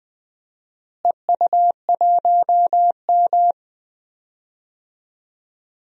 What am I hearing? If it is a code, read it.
Morse code, heard as a keyed tone, about 20 wpm, 700 Hz. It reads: EU1M
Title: DXCC-100 Call Signs - 4 Characters - Repeated 2x - 20wpm